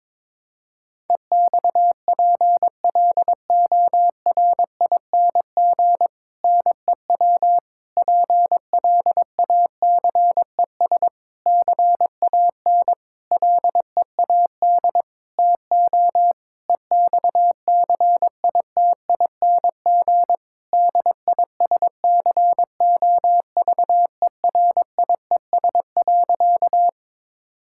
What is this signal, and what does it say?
Morse code, heard as a keyed tone, about 22 wpm, 700 Hz. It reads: EXPLORING NEW PLACES CAN LEAD TO EXCITING DISCOVERIES.
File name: Difficult Sentences with Hints - 22wpm